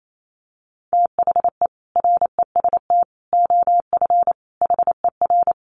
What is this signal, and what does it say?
Morse code, heard as a keyed tone, about 28 wpm, 700 Hz. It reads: THE REST OF HER
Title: Sets of 5 Words Encore 28wpm